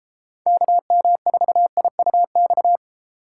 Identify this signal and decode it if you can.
Morse code, heard as a keyed tone, about 32 wpm, 700 Hz. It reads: KM4IUX